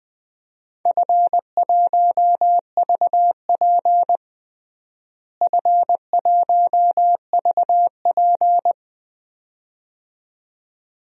Morse code, heard as a keyed tone, about 20 wpm, 700 Hz. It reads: F1VP F1VP
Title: DXCC-100 Call Signs - 4 Characters - Repeated 2x - 20wpm